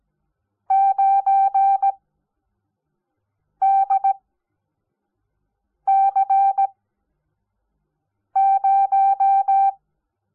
Morse code, heard as a keyed tone, about 17 wpm, 780 Hz. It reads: 9 D C 0
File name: Onlinelehrgang / morsen / DJ4UF Morsen05